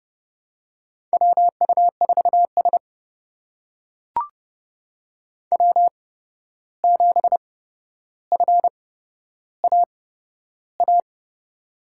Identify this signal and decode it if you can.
Morse code, heard as a keyed tone, about 30 wpm, spaced as for 12 wpm, 700 Hz. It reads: WU4S W7FAA
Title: USA Call Signs - 8x Character Spacing - 30wpm